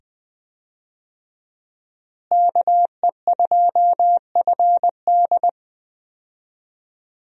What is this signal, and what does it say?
Morse code, heard as a keyed tone, about 20 wpm, 700 Hz. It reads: KE2FD
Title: USA Call Signs - 5 Characters - Repeated 2x - 20wpm